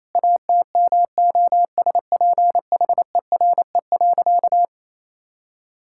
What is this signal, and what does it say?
Morse code, heard as a keyed tone, about 28 wpm, 700 Hz. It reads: ATMOSPHERE.
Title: Sentences from Top 1600 Words 28wpm